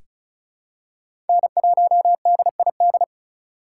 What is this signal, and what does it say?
Morse code, heard as a keyed tone, about 36 wpm, 700 Hz. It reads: N1DID
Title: K1USN SST QSO Practice - 35wpm